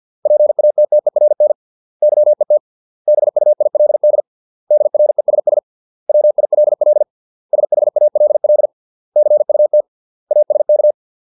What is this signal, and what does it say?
Morse code, heard as a keyed tone, about 50 wpm, 600 Hz. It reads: PATTERN YET BUILD DRESS WILL SHALL CUT ASK